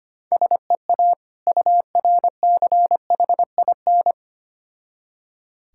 Morse code, heard as a keyed tone, about 25 wpm, 700 Hz. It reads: SEA URCHIN
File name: practice / 25wpm / Difficult Words with Hints - 25wpm